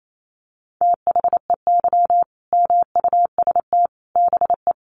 Morse code, heard as a keyed tone, about 28 wpm, 700 Hz. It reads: THEY MUST BE
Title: Sentences from Top 1400 Words 28wpm